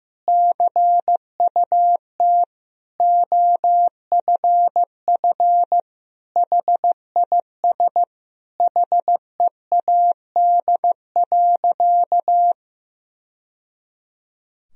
Morse code, heard as a keyed tone, about 14 wpm, 700 Hz. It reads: CUT OFF HIS HEAD.
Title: Sentences from Top 1000 Words - Review Part 1 - 15wpm